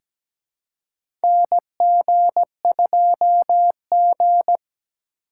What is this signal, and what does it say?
Morse code, heard as a keyed tone, about 17 wpm, 700 Hz. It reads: NG2G
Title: USA Call Signs - 4 Characters - 4x Character Spacing - 17wpm